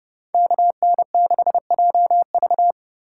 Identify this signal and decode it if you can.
Morse code, heard as a keyed tone, about 28 wpm, 700 Hz. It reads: KN6JV